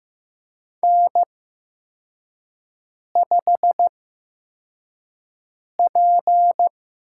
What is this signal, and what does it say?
Morse code, heard as a keyed tone, about 15 wpm, spaced as for 4 wpm, 700 Hz. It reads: N5P
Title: USA Call Signs - 5 Characters - 8x Character Spacing - 15wpm